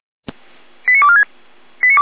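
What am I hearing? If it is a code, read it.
Five-tone selective call (ZVEI), heard as quick runs of five tones, tones of about 70 ms. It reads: 89246
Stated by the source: Five-tone ZVEI selective call received by SDR.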